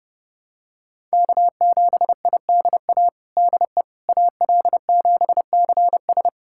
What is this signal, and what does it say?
Morse code, heard as a keyed tone, about 30 wpm, 700 Hz. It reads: K7IDA DE AL7CS
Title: USA Callsign DE Callsign - 5 Characters - 30wpm